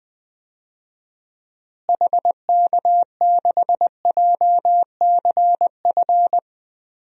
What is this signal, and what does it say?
Morse code, heard as a keyed tone, about 20 wpm, 700 Hz. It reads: HK6JCF
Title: DXCC-100 Call Signs - 6 Characters - Repeated 2x - 20wpm